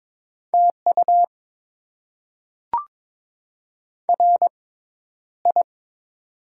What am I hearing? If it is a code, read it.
Morse code, heard as a keyed tone, about 22 wpm, spaced as for 11 wpm, 700 Hz. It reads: TU RI
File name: Top 20 QSO Elements - 6x Character Spacing - 22wpm